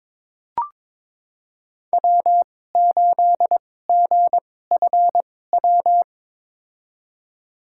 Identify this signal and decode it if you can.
Morse code, heard as a keyed tone, about 22 wpm, spaced as for 16 wpm, 700 Hz. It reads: W8GFW
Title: USA Call Signs - 5 Characters - 2x Character Spacing - 22wpm